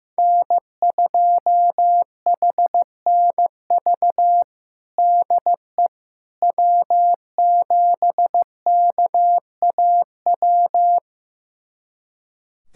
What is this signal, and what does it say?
Morse code, heard as a keyed tone, about 15 wpm, 700 Hz. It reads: N2HNV DE W7KAW
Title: USA Callsign DE Callsign - 5 Characters - 15wpm